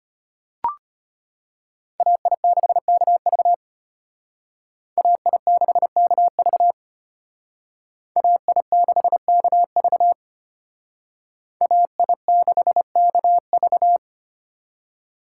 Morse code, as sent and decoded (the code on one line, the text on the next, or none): .- .. -.... -.- ...- / .- .. -.... -.- ...- / .- .. -.... -.- ...- / .- .. -.... -.- ...-
AI6KV AI6KV AI6KV AI6KV